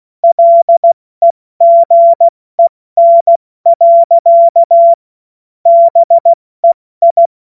Morse code, heard as a keyed tone, about 16 wpm, 680 Hz. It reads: LEGEN. BEI
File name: Morsekurs / Witze / Witz 06. September 2022 70 F80